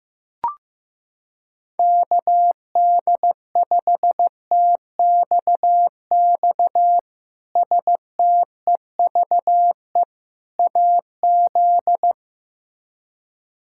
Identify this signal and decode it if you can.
Morse code, heard as a keyed tone, about 16 wpm, 700 Hz. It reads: KD5TXX STEVE AZ
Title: K1USN SST QSO Practice - 15wpm